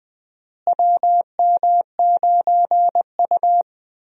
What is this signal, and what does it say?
Morse code, heard as a keyed tone, about 20 wpm, 700 Hz. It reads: WM9U
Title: USA Call Signs - 4 Characters - Repeated 2x - 20wpm